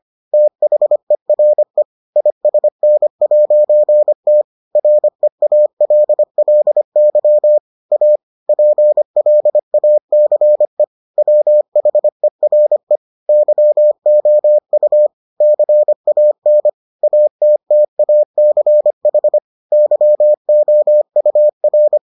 Morse code, heard as a keyed tone, about 25 wpm, 600 Hz. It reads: THERE ISN'T REALLY A PLACE WHERE YOU CAN ATTACH YOUR